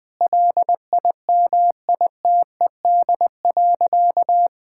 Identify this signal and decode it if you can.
Morse code, heard as a keyed tone, about 20 wpm, 700 Hz. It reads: LIMITED.
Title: Sentences from Top 1000 Words - Review Part 6 - 20wpm